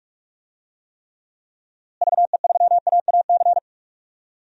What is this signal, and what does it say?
Morse code, heard as a keyed tone, about 46 wpm, 700 Hz. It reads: VE3AAC
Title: DXCC-100 Call Signs - 6 Characters - Repeated 3x - 45wpm